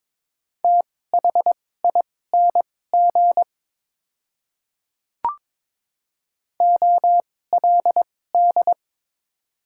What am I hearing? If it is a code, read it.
Morse code, heard as a keyed tone, about 22 wpm, spaced as for 16 wpm, 700 Hz. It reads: THING OLD
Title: 2x Character Spacing - Top 300 Words - 22wpm